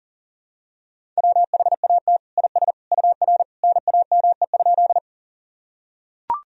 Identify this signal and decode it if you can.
Morse code, heard as a keyed tone, about 40 wpm, 700 Hz. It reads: WHAT IS UR NAME?
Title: Sentences from Top 200 Words and Abbreviations 40wpm